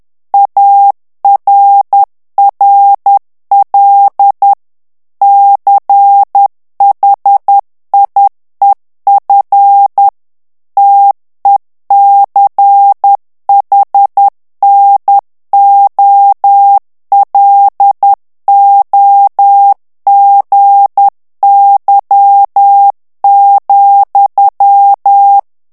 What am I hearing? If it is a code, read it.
Morse code, heard as a keyed tone, about 10 wpm, 800 Hz. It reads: ARRL CHIEF TECHNOLOGY,
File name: ANGLAIS texte clair 3 vitesse 10 mots minute